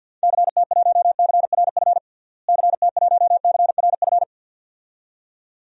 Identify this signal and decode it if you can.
Morse code, heard as a keyed tone, about 52 wpm, 700 Hz. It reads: CT1CRF CT1CRF
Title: DXCC-100 Call Signs - 6 Characters - Repeated 2x - 50wpm